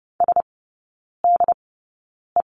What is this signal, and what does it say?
Morse code, heard as a keyed tone, about 30 wpm, spaced as for 9 wpm, 700 Hz. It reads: SDE
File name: 081 Lesson 14e - Number 2 - Three Characters 30wpm